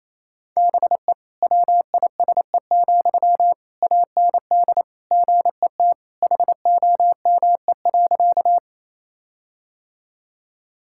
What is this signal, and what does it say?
Morse code, heard as a keyed tone, about 28 wpm, 700 Hz. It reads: BE WISE, AND GET HOME.
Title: Sentences from Top 1600 Words 28wpm